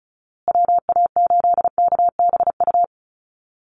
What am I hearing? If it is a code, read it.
Morse code, heard as a keyed tone, about 34 wpm, 700 Hz. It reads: WA8KBU
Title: Call Signs Repeated 2x - 35wpm